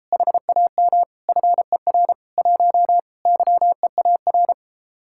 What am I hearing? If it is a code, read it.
Morse code, heard as a keyed tone, about 33 wpm, 700 Hz. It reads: HAM FER 1 YEAR